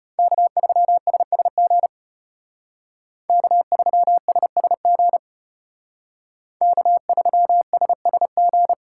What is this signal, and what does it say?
Morse code, heard as a keyed tone, about 35 wpm, 700 Hz. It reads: K3SSG K3SSG K3SSG